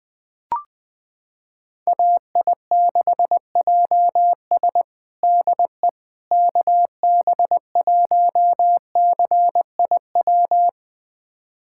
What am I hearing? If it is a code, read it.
Morse code, heard as a keyed tone, about 20 wpm, 700 Hz. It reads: AI6JS DE KB1CIW
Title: USA Callsign DE Callsign 20wpm